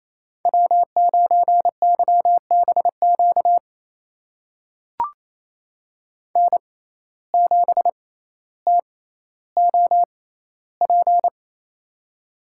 Morse code, heard as a keyed tone, about 28 wpm, spaced as for 14 wpm, 700 Hz. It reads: W9YBQ N7TOP
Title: USA Call Signs - 5 Characters - 6x Character Spacing - 28wpm